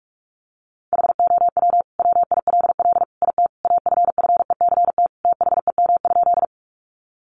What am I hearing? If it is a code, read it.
Morse code, heard as a keyed tone, about 44 wpm, 700 Hz. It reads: HOW WILL IT AFFECT THEM?
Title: Sentences from Top 900 Words 45wpm